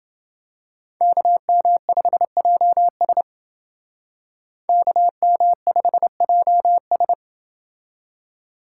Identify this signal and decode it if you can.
Morse code, heard as a keyed tone, about 29 wpm, 700 Hz. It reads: KM5JS KM5JS